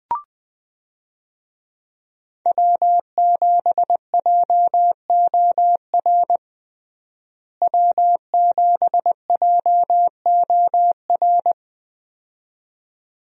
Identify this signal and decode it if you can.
Morse code, heard as a keyed tone, about 20 wpm, 700 Hz. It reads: W7JOR W7JOR
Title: USA Call Signs - 5 Characters - Repeated 2x - 20wpm